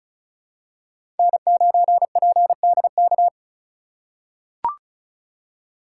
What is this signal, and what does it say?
Morse code, heard as a keyed tone, about 36 wpm, 700 Hz. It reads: N9PDK
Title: USA Call Signs - 5 Characters - 3x Character Spacing - 35wpm